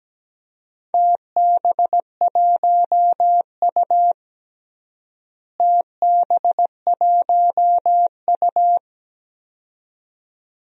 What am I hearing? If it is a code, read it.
Morse code, heard as a keyed tone, about 17 wpm, 700 Hz. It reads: TB1U TB1U
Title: DXCC-100 Call Signs - 4 Characters - Repeated 2x - 17wpm